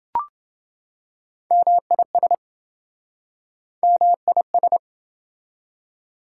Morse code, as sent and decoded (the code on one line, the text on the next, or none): -- .. ... / -- .. ...
MIS MIS